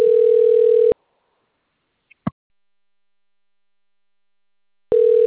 Ringback tone (North America), 440 + 480 Hz.